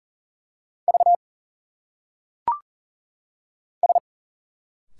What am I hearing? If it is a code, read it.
Morse code, heard as a keyed tone, about 40 wpm, 700 Hz. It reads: V S
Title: Single Letters - Phonetic Pronunciation 40wpm